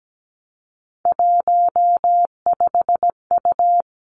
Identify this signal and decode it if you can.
Morse code, heard as a keyed tone, about 17 wpm, 700 Hz. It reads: 15U
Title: Three Challenging Letter-Number Combinations 17wpm